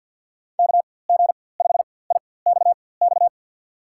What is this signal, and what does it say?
Morse code, heard as a keyed tone, about 53 wpm, spaced as for 24 wpm, 700 Hz. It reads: KG5IXX
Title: USA Call Signs - 6 Characters - 4x Character Spacing - 50wpm